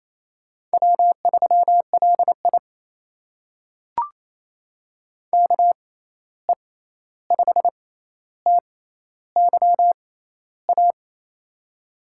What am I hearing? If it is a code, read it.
Morse code, heard as a keyed tone, about 27 wpm, spaced as for 13 wpm, 700 Hz. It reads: W3LI KE5TYA
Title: USA Call Signs - 6x Character Spacing - 28wpm